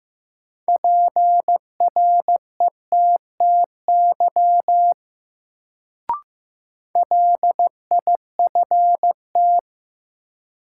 Morse code, heard as a keyed tone, about 15 wpm, 700 Hz. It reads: PRETTY LIFT
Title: Sentences from Top 1000 Words - New Word Drill - 15wpm